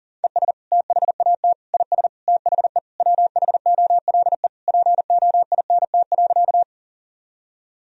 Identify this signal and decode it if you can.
Morse code, heard as a keyed tone, about 37 wpm, 700 Hz. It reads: ES THAT IS THE WHOLE POINT.
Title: Sentences from Top 400 Words and Abbreviations 40wpm